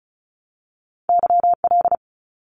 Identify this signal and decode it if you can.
Morse code, heard as a keyed tone, about 34 wpm, 700 Hz. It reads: YL
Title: Abbreviations 35wpm